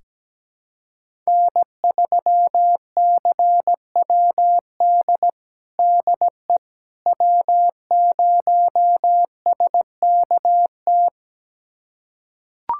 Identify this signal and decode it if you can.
Morse code, heard as a keyed tone, about 17 wpm, 700 Hz. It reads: N3CWD DE W0SKT
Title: USA Callsign DE Callsign - 5 Characters - 17wpm